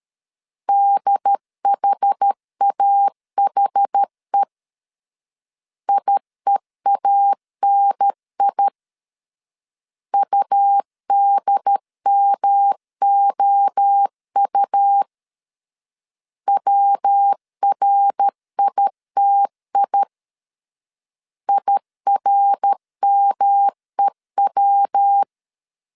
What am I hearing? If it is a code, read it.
Morse code, heard as a keyed tone, about 13 wpm, 800 Hz. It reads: DHAHE IEANI UDMOU WRITI IRMEW
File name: CW / DET / 1 3 1500C S C 12wpm 2750